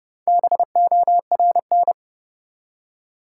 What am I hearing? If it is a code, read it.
Morse code, heard as a keyed tone, about 29 wpm, 700 Hz. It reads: BORN
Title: Four Letter Words - 30wpm